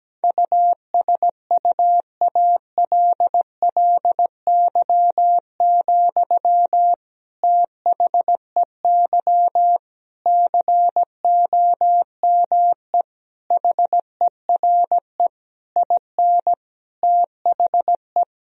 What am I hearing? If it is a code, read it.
Morse code, heard as a keyed tone, about 17 wpm, 700 Hz. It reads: USUALLY, THEY COME HERE IN THE
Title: Sentences from Top 1000 Words - Review Part 3 - 17wpm